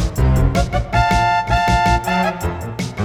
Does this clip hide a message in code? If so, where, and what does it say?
No.